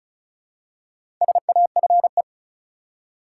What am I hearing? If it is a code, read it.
Morse code, heard as a keyed tone, about 34 wpm, 700 Hz. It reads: SAFE
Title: Four Letter Words - 35wpm